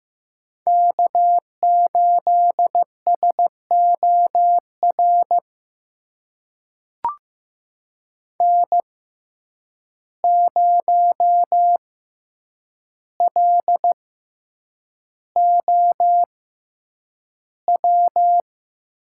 Morse code, heard as a keyed tone, about 15 wpm, spaced as for 8 wpm, 700 Hz. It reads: K8SOR N0LOW
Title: USA Call Signs - 5 Characters - 6x Character Spacing - 15wpm